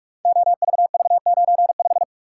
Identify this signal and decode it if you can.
Morse code, heard as a keyed tone, about 44 wpm, 700 Hz. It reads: OVV95
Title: Code Groups - 5 Letter:Numbers - 45wpm